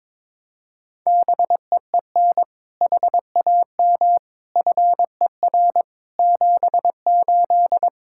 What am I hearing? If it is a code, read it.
Morse code, heard as a keyed tone, about 22 wpm, 700 Hz. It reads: BEEN HAM FER 78
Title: K1USN SST QSO Practice 22wpm